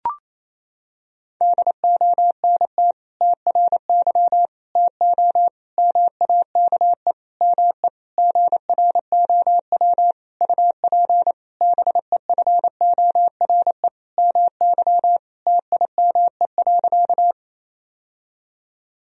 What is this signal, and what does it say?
Morse code, heard as a keyed tone, about 28 wpm, 700 Hz. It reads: DONT TRY TO MAKE ME GROW UP BEFORE MY TIME.
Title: Sentences from Top 1000 Words - Review Part 1 - 28wpm